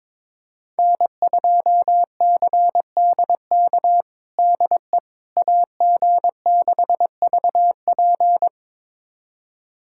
Morse code, heard as a keyed tone, about 22 wpm, 700 Hz. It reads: N2CDK DE AG6VP
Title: USA Callsign DE Callsign - 5 Characters - 22wpm